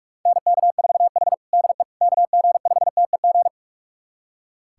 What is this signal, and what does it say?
Morse code, heard as a keyed tone, about 46 wpm, 690 Hz. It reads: NK4H DE KG5TEG
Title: USA Callsign DE Callsign 45wpm